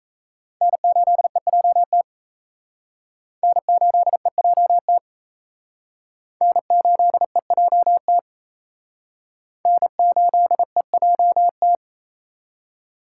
Morse code, heard as a keyed tone, about 34 wpm, 700 Hz. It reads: N8EJT N8EJT N8EJT N8EJT
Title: Speed Racing - USA Call Signs 28wpm